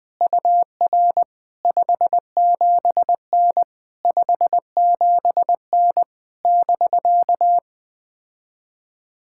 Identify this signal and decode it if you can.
Morse code, heard as a keyed tone, about 20 wpm, 700 Hz. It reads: UR 57N 57N <BK>
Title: POTA QSO Practice - 20wpm